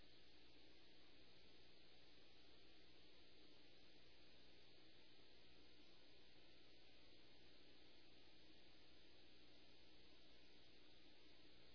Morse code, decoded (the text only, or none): none